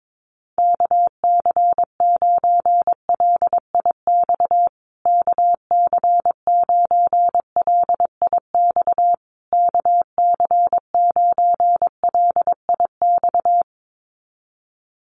Morse code, heard as a keyed tone, about 22 wpm, 700 Hz. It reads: KC9LIX KC9LIX KC9LIX
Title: USA Call Signs Repeated 3x 22wpm